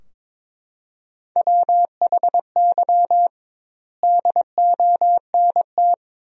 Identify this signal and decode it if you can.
Morse code, heard as a keyed tone, about 22 wpm, 700 Hz. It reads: WHY DONT
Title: 2x Word-Spacing - Sentences from Top 300 Words - 22wpm